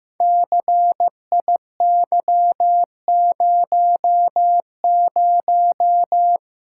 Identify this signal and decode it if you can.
Morse code, heard as a keyed tone, about 15 wpm, 700 Hz. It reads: CIY00
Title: Code Groups - 5 Letter:Numbers - 15wpm